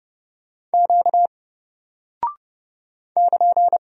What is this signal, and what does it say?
Morse code, heard as a keyed tone, about 31 wpm, 700 Hz. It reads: Q <KN>
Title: Single Character (Letter:Number:Punctuation:Prosign) 30wpm